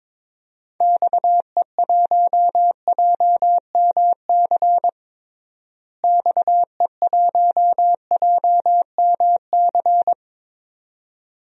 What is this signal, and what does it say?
Morse code, heard as a keyed tone, about 22 wpm, 700 Hz. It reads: XE1JMC XE1JMC
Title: DXCC-100 Call Signs - 6 Characters - Repeated 2x - 22wpm